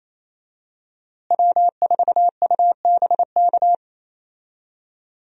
Morse code, heard as a keyed tone, about 28 wpm, 700 Hz. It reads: W4UBK
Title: USA Call Signs - 5 Characters - Repeated 3x - 28wpm